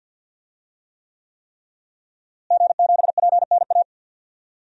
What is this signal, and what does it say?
Morse code, heard as a keyed tone, about 51 wpm, 700 Hz. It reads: G7PNA
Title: DXCC-100 Call Signs - 5 Characters - Repeated 3x - 50wpm